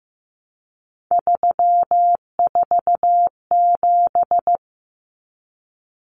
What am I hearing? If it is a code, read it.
Morse code, heard as a keyed tone, about 15 wpm, 700 Hz. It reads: 347
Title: Numbers - 3-Digit 15wpm